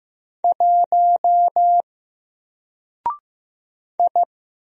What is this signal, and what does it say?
Morse code, heard as a keyed tone, about 15 wpm, 700 Hz. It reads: 1 I
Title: Single Letter-Number 15wpm